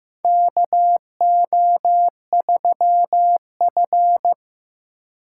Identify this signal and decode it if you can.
Morse code, heard as a keyed tone, about 15 wpm, 700 Hz. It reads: KO3F
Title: USA Call Signs - 4 Characters - 6x Character Spacing - 15wpm